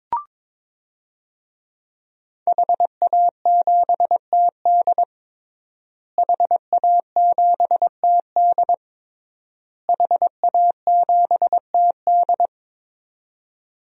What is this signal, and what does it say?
Morse code, heard as a keyed tone, about 22 wpm, 700 Hz. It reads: HA7TD HA7TD HA7TD
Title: DXCC-100 Call Signs - 5 Characters - Repeated 3x - 22wpm